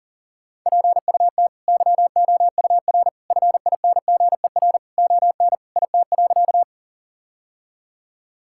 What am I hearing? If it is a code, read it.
Morse code, heard as a keyed tone, about 40 wpm, 700 Hz. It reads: PUT YOUR FINGER ON IT.
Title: Sentences from Top 1600 Words 40wpm